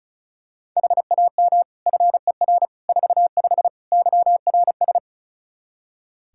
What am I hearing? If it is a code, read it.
Morse code, heard as a keyed tone, about 36 wpm, 700 Hz. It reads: HAM FER 45 YRS
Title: K1USN SST QSO Practice 35wpm